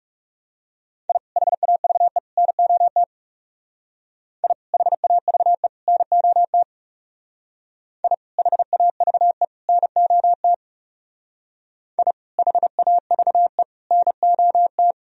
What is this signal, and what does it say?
Morse code, heard as a keyed tone, about 36 wpm, 700 Hz. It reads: I HAVE NOT I HAVE NOT I HAVE NOT I HAVE NOT
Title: Speed-Racing - Sets of 3 Words 30wpm